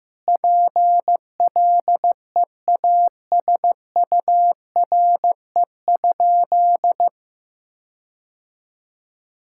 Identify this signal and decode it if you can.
Morse code, heard as a keyed tone, about 16 wpm, 700 Hz. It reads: PLEASURE?
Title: Sentences from Top 1600 Words 15wpm